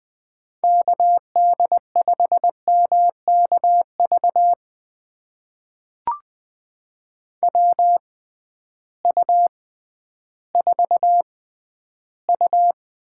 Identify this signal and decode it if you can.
Morse code, heard as a keyed tone, about 20 wpm, spaced as for 12 wpm, 700 Hz. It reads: KD5MKV WU4U
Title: USA Call Signs - 6x Character Spacing - 20wpm